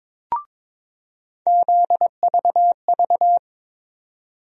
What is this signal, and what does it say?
Morse code, heard as a keyed tone, about 22 wpm, 700 Hz. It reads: ZVV
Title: USA Call Sign Postfixes - 3 Letters 22wpm